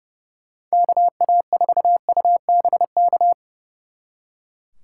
Morse code, heard as a keyed tone, about 36 wpm, 700 Hz. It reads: KA4UBK
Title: Speed Racing - USA 2x3 Call Signs 30wpm